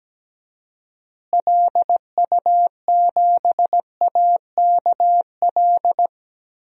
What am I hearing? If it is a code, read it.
Morse code, heard as a keyed tone, about 17 wpm, 700 Hz. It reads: LU7AKL
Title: DXCC-100 Call Signs - 6 Characters - Repeated 3x - 17wpm